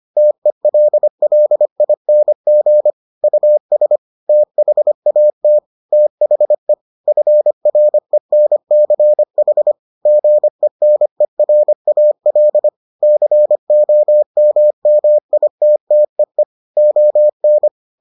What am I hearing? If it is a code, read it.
Morse code, heard as a keyed tone, about 25 wpm, 600 Hz. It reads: TELLING US THAT THE FRENCH GENERAL COMMITTEE ON